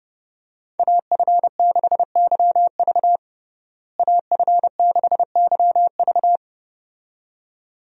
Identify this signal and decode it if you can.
Morse code, heard as a keyed tone, about 29 wpm, 700 Hz. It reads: AF6YV AF6YV
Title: USA Call Signs - 5 Characters - Repeated 2x - 30wpm